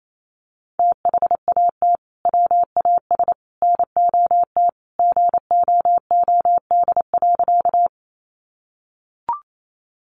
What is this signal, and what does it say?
Morse code, heard as a keyed tone, about 28 wpm, 700 Hz. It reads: THAT WAS NOT GOOD.
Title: Sentences from Top 600 Words 28wpm